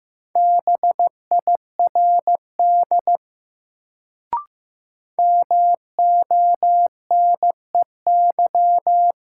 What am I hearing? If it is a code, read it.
Morse code, heard as a keyed tone, about 15 wpm, 700 Hz. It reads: BIRD MONEY
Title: Sentences from Top 400 Words - New Word Drill - 15wpm